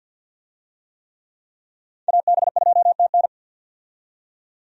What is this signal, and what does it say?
Morse code, heard as a keyed tone, about 52 wpm, 700 Hz. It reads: AB2TN
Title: USA Call Signs - 5 Characters - 50wpm